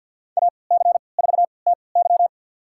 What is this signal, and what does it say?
Morse code, heard as a keyed tone, about 49 wpm, spaced as for 28 wpm, 700 Hz. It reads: AC4TY
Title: USA Call Signs - 5 Characters - 3x Character Spacing - 50wpm